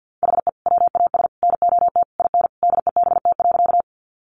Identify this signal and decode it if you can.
Morse code, heard as a keyed tone, about 50 wpm, 700 Hz. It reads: HE WAS NOT IN DEBT.
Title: Sentences from Top 1300 Words 50wpm